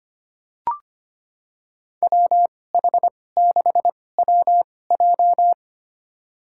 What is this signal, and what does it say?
Morse code, heard as a keyed tone, about 25 wpm, spaced as for 18 wpm, 700 Hz. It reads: WH6WJ
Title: USA Call Signs - 5 Characters - 2x Character Spacing - 25wpm